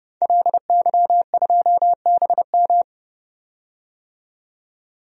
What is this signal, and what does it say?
Morse code, heard as a keyed tone, about 32 wpm, 700 Hz. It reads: LY2BM
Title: DXCC-100 Call Signs - 5 Characters - 30wpm